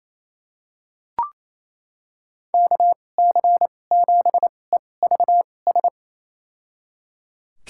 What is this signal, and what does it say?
Morse code, heard as a keyed tone, about 28 wpm, spaced as for 20 wpm, 700 Hz. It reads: KC7EVS
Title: USA Call Signs - 2x Character Spacing - 28wpm